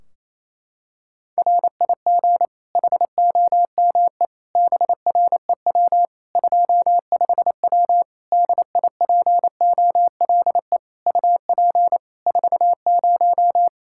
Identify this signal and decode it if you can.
Morse code, heard as a keyed tone, about 28 wpm, 700 Hz. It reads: RIG HOME BREW 25W DIPOLE UP 40
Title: K1USN SST QSO Practice 28wpm